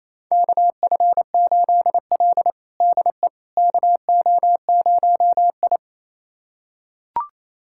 Morse code, heard as a keyed tone, about 28 wpm, 700 Hz. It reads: KF8L DE KO0I
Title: USA Callsign DE Callsign - 4 Characters - 28wpm